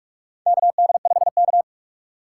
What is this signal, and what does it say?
Morse code, heard as a keyed tone, about 45 wpm, 700 Hz. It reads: KD5K
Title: USA Call Signs - 2x Character Spacing - 45wpm